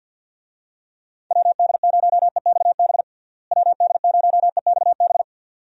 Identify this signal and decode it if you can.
Morse code, heard as a keyed tone, about 51 wpm, 700 Hz. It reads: WD0EXB WD0EXB
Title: USA Call Signs - 6 Characters - Repeated 2x - 50wpm